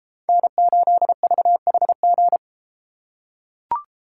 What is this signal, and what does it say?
Morse code, heard as a keyed tone, about 33 wpm, 700 Hz. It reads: N8VHG